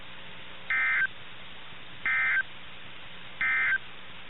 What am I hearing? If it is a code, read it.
Emergency Alert System (SAME) tones, heard as harsh buzzing bursts. It is heard from 0.6 to 3.9 s.
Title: NOAA Weather Radio - KIH61